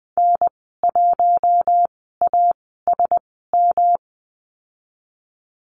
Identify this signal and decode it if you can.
Morse code, heard as a keyed tone, about 20 wpm, spaced as for 15 wpm, 700 Hz. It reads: N1ASM
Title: 186 Lesson 30g - Letter b - Call Signs 20wpm